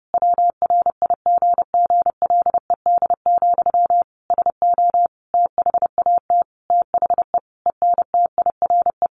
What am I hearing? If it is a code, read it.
Morse code, heard as a keyed tone, about 30 wpm, 700 Hz. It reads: WRIGGLED, SO THAT THE ENTIRE
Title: A Sound Of Thunder 30wpm